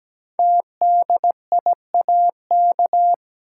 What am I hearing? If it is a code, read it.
Morse code, heard as a keyed tone, about 17 wpm, 700 Hz. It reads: TDIAK
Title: Code Groups - 5 Letters - 17wpm